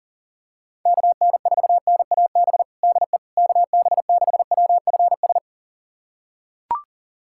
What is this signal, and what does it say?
Morse code, heard as a keyed tone, about 37 wpm, 700 Hz. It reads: KN4NAB DE KB6WFS
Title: USA Callsign DE Callsign - 6 Characters - 40wpm